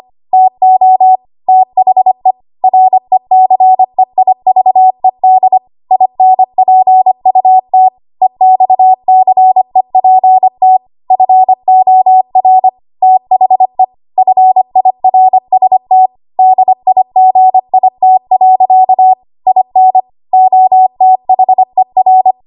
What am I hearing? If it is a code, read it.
Morse code, heard as a keyed tone, about 25 wpm, 750 Hz. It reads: TO THE RECEIVED INPUT EXCEPT FOR THE FIRST DIGIT. IN OTHER
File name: Archive / 25 WPM / 25WPM